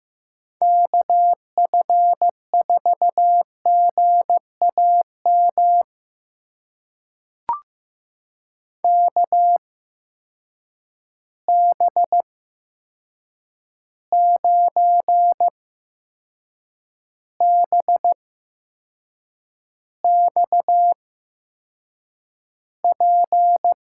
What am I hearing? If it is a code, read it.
Morse code, heard as a keyed tone, about 15 wpm, spaced as for 6 wpm, 700 Hz. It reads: KF4GAM KB9BXP
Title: USA Call Signs - 6 Characters - 8x Character Spacing - 15wpm